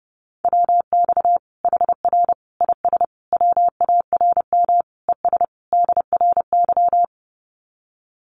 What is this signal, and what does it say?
Morse code, heard as a keyed tone, about 30 wpm, 700 Hz. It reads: WX HR IS WARM ES DRY
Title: Common QSO Phrases 30wpm